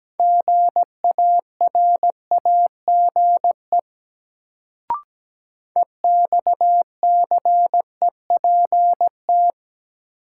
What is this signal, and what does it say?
Morse code, heard as a keyed tone, about 17 wpm, 700 Hz. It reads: GARAGE EXCEPT
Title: Sentences from Top 1600 Words - New Word Drill - 17wpm